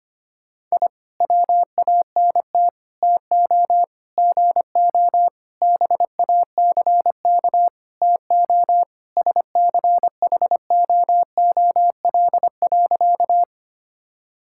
Morse code, heard as a keyed tone, about 25 wpm, 700 Hz. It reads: I WANT TO GO BACK TO SCHOOL.
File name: practice / 25wpm / Sentences with 4-Word Phrases - 25wpm